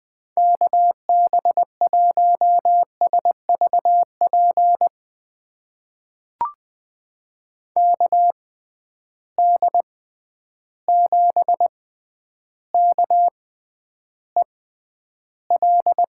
Morse code, heard as a keyed tone, about 20 wpm, spaced as for 10 wpm, 700 Hz. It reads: KB1SVP KD7KEL
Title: USA Call Signs - 6x Character Spacing - 20wpm